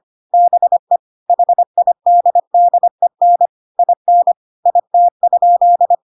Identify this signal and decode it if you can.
Morse code, heard as a keyed tone, about 25 wpm, 700 Hz. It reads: BE HIDDEN IN IT?
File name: Pinocchio / Pinocchio ch1 25 25 ik1hge